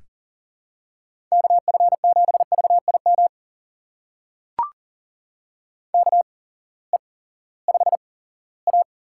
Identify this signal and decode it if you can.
Morse code, heard as a keyed tone, about 40 wpm, spaced as for 20 wpm, 700 Hz. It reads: KF7VIM KE5A
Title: USA Call Signs - 6 Characters - 8x Character Spacing - 40wpm